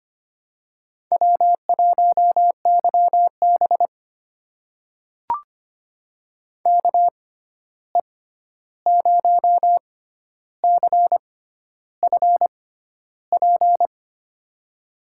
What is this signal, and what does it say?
Morse code, heard as a keyed tone, about 25 wpm, spaced as for 11 wpm, 700 Hz. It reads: W1YB KE0CFP
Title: USA Call Signs - 6x Character Spacing - 25wpm